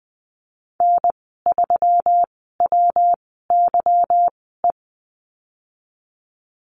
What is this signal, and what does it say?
Morse code, heard as a keyed tone, about 20 wpm, spaced as for 15 wpm, 700 Hz. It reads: N3WYE
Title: 154 Lesson 25g - Letter p - Call Signs 20wpm